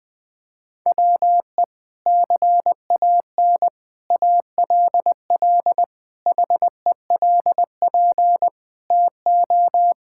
Morse code, heard as a keyed tone, about 21 wpm, 700 Hz. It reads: WE CAN ALL HELP TO
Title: Sentences with 6-Word Phrases - 20wpm